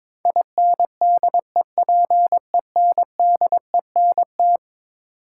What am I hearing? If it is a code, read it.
Morse code, heard as a keyed tone, about 21 wpm, 700 Hz. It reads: INDEPENDENT
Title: Sentences from Top 900 Words - New Word Drill - 22wpm